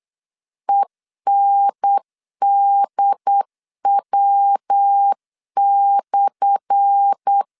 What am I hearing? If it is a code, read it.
Morse code, heard as a keyed tone, about 8 wpm, 800 Hz. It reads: ENDW/
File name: LEZIONI / 10 SIGNS / 10A SLASH 8wpm 3452